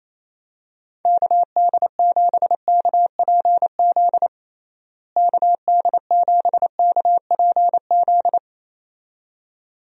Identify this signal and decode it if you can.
Morse code, heard as a keyed tone, about 27 wpm, 700 Hz. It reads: KD7KPZ KD7KPZ
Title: USA Call Signs - 6 Characters - Repeated 2x - 28wpm